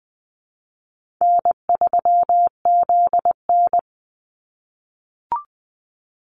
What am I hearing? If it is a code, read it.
Morse code, heard as a keyed tone, about 20 wpm, 700 Hz. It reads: N3ZN
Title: Easy Call Signs - 20wpm